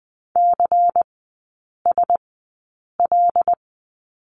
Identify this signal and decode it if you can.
Morse code, heard as a keyed tone, about 20 wpm, spaced as for 8 wpm, 700 Hz. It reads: CSL
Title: 093 Lesson 16e - Letter c - Three Characters 20wpm